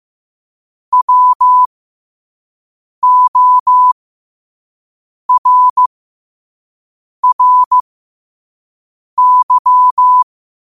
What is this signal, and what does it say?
Morse code, heard as a keyed tone, about 15 wpm, spaced as for 5 wpm, 1000 Hz. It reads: WORRY
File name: qotdfiles / 5wpm